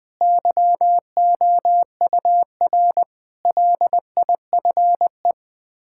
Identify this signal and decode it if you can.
Morse code, heard as a keyed tone, about 20 wpm, 700 Hz. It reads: YOUR LIFE
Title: Build A Phrase - 5-Word Phrases - 20wpm